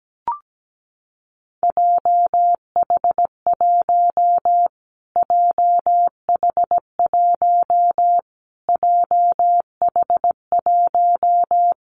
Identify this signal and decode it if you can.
Morse code, heard as a keyed tone, about 17 wpm, 700 Hz. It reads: JH1 JH1 JH1
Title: Callsign Prefixes for Top 25 DXCC - Repeated 3x - 17wpm